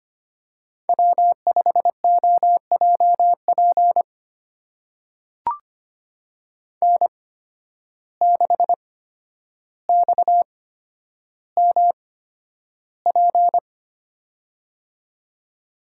Morse code, heard as a keyed tone, about 25 wpm, spaced as for 11 wpm, 700 Hz. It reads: W5OJP N6XMP
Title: USA Call Signs - 5 Characters - 8x Character Spacing - 25wpm